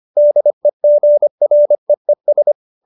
Morse code, heard as a keyed tone, about 25 wpm, 590 Hz. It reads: DEGREES